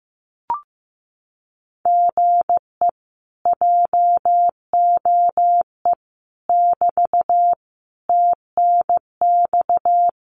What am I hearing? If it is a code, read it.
Morse code, heard as a keyed tone, about 15 wpm, 700 Hz. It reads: GE JOE <BT> TNX
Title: Common QSO Phrases Encore 15wpm